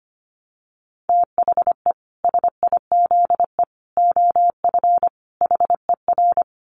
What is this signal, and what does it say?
Morse code, heard as a keyed tone, about 25 wpm, 700 Hz. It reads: THE SIZE OF HER
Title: Dit Challenge 25wpm